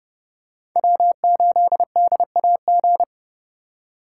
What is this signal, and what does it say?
Morse code, heard as a keyed tone, about 29 wpm, 700 Hz. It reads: W8DAG